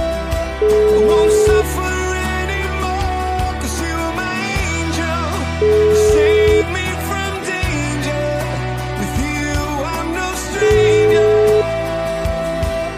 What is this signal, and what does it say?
Ringback tone (Europe, 425 Hz), with noise.